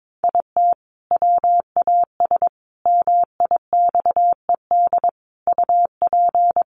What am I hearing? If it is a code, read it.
Morse code, heard as a keyed tone, about 22 wpm, 700 Hz. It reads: IT WAS MIXED UP
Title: Sentences from Top 1500 Words 22wpm